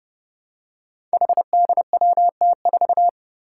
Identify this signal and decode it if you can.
Morse code, heard as a keyed tone, about 30 wpm, 700 Hz. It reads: HDWT4
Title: Five Challenging Letter-Number Combinations 30wpm